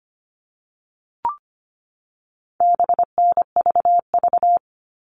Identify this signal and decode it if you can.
Morse code, heard as a keyed tone, about 25 wpm, 700 Hz. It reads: BNVV
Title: All Cut Numbers - 4-Digit 25wpm